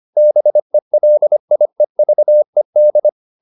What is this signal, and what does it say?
Morse code, heard as a keyed tone, about 25 wpm, 600 Hz. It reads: BELIEVED